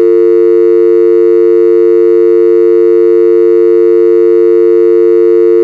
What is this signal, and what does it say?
Dial tone (UK), 350 + 450 Hz.
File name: UK Dial tone sound